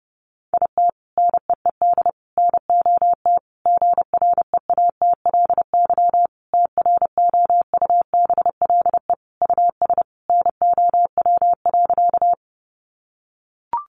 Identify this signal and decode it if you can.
Morse code, heard as a keyed tone, about 30 wpm, 700 Hz. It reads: IT NEED NOT GREATLY TROUBLE US NOW.
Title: Sentences from Top 1400 Words 30wpm